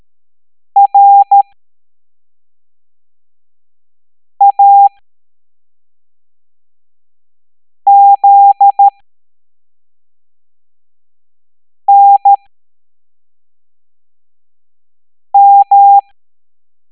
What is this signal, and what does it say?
Morse code, heard as a keyed tone, about 13 wpm, spaced as for 3 wpm, 800 Hz. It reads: RAZNM